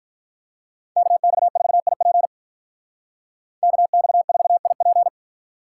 Morse code, heard as a keyed tone, about 43 wpm, 700 Hz. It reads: KX4IP KX4IP
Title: Speed Racing - USA 2x2 Call Signs 35wpm